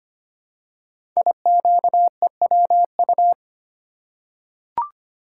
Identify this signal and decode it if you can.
Morse code, heard as a keyed tone, about 25 wpm, 700 Hz. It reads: IQEWU
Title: Code Groups - 5 Letters - 25wpm